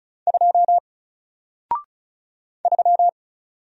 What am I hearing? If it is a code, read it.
Morse code, heard as a keyed tone, about 36 wpm, 700 Hz. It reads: 2 3
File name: Single-Letter-Number-35wpm